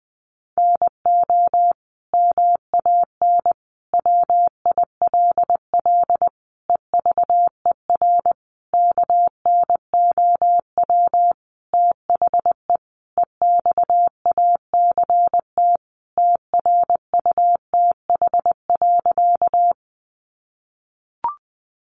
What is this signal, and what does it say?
Morse code, heard as a keyed tone, about 20 wpm, 700 Hz. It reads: NO MAN WILL EVER KNOW THE EXACT TRUTH.
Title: Sentences from Top 1300 Words 20wpm